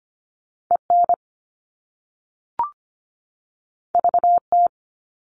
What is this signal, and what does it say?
Morse code, heard as a keyed tone, about 25 wpm, 700 Hz. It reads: EN VT
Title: All Cut Numbers - 2-Digit 25wpm